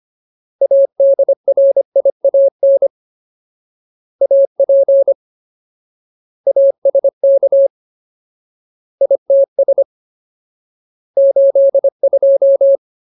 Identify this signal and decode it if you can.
Morse code, heard as a keyed tone, about 25 wpm, 550 Hz. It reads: ADRIAN AP ASK ITS 82
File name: Ultimate Word List / 25WPM Ultimate